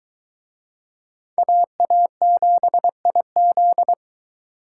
Morse code, heard as a keyed tone, about 21 wpm, 700 Hz. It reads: AA7IZ